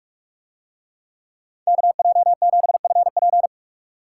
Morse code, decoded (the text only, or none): KJ7FP